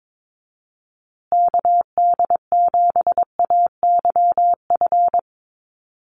Morse code, heard as a keyed tone, about 22 wpm, 700 Hz. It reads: KD7AYF
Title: USA Call Signs Repeated 3x 22wpm